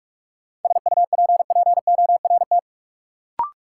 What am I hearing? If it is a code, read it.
Morse code, heard as a keyed tone, about 46 wpm, 700 Hz. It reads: SUPPORT
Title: Sentences from Top 800 Words - New Word Drill - 45wpm